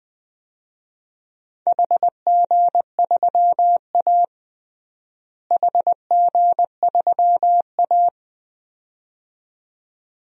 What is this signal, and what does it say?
Morse code, heard as a keyed tone, about 20 wpm, 700 Hz. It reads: HG3A HG3A
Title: DXCC-100 Call Signs - 4 Characters - Repeated 2x - 20wpm